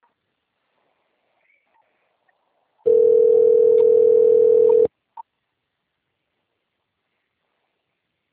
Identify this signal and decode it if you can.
Ringback tone (North America), 440 + 480 Hz.
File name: out- - - - - .65898